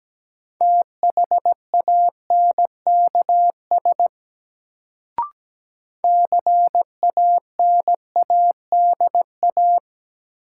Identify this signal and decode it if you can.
Morse code, heard as a keyed tone, about 17 wpm, 700 Hz. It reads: THANKS CANADA
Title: Sets of 5 Words - New Word Drill - 17wpm